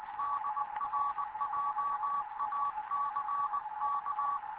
Morse code, heard as a keyed tone, about 20 wpm, 1140 Hz, with noise. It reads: DR JACK
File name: Morze-HA2MN / hangp / 2 11 GYAKORLAT